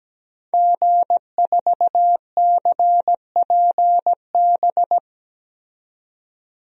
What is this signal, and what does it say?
Morse code, heard as a keyed tone, about 17 wpm, 700 Hz. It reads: G4CPB
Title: DXCC-100 Call Signs - 5 Characters - 17wpm